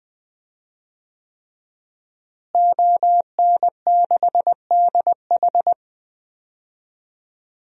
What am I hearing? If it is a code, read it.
Morse code, heard as a keyed tone, about 21 wpm, 700 Hz. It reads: ON6DH
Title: DXCC-100 Call Signs - 5 Characters - Repeated 3x - 20wpm